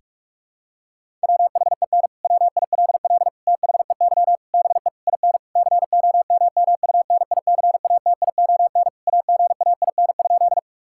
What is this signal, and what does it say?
Morse code, heard as a keyed tone, about 46 wpm, 700 Hz. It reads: WHEN WILL THEY BE IN COMMUNICATION AGAIN?
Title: Sentences from Top 1000 Words - Review Part 6 - 45wpm